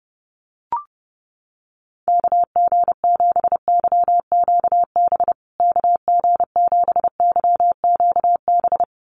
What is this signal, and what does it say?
Morse code, heard as a keyed tone, about 30 wpm, 700 Hz. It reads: KG7YQB KG7YQB
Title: USA Call Signs Repeated 2x 30wpm